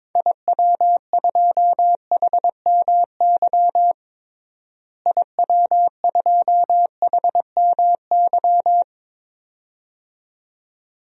Morse code, heard as a keyed tone, about 22 wpm, 700 Hz. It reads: IW2HMY IW2HMY
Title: DXCC-100 Call Signs - 6 Characters - Repeated 2x - 22wpm